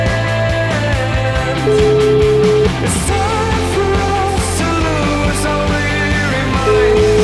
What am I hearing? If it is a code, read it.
Ringback tone (Europe, 425 Hz), with noise.